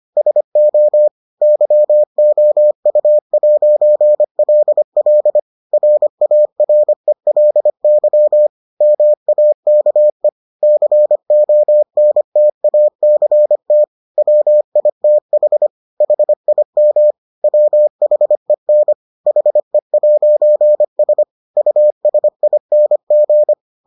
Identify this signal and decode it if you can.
Morse code, heard as a keyed tone, about 25 wpm, 600 Hz. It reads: SO YOU'LL RARELY MAKE CONTACT WITH HIM WHEN HE'S USING